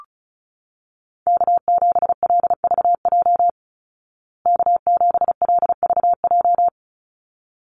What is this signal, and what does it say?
Morse code, heard as a keyed tone, about 34 wpm, 700 Hz. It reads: K7LVJ K7LVJ
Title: Call Signs Repeated 3x - 35wpm